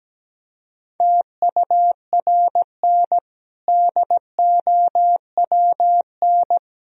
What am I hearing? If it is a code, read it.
Morse code, heard as a keyed tone, about 17 wpm, 700 Hz. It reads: TURN DOWN
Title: Sentences from Top 1000 Words - Review Part 3 - 17wpm